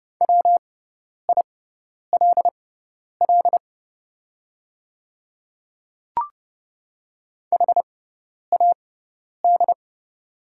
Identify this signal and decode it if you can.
Morse code, heard as a keyed tone, about 29 wpm, spaced as for 10 wpm, 690 Hz. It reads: WILL HAD
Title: 6x Character Spacing - Top 100 Words - 30wpm